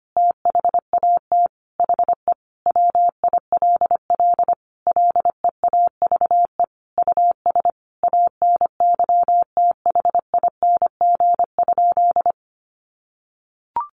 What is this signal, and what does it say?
Morse code, heard as a keyed tone, about 25 wpm, 690 Hz. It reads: THAT HE WILL LEAVE US ANYTHING?
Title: Sentences from Top 1100 Words 25wpm